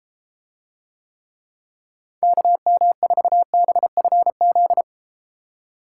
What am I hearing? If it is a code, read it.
Morse code, heard as a keyed tone, about 34 wpm, 700 Hz. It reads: KM4BFZ